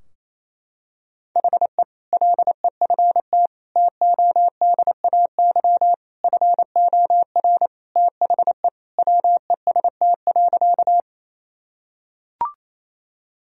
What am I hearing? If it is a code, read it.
Morse code, heard as a keyed tone, about 27 wpm, 700 Hz. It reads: HE LEFT TODAY FOR THE WEST.
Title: Sentences from Top 1000 Words - Review Part 2 - 28wpm